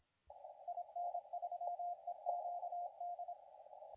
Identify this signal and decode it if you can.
Morse code, heard as a keyed tone, about 28 wpm, 700 Hz, with radio noise. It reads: IN3EQD
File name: la4c / sac-cw-2017 / CONTEST AUDIO 2100Z